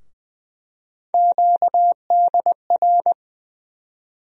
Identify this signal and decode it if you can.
Morse code, heard as a keyed tone, about 20 wpm, 700 Hz. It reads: QDR
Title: USA Call Sign Postfixes - 3 Letters 20wpm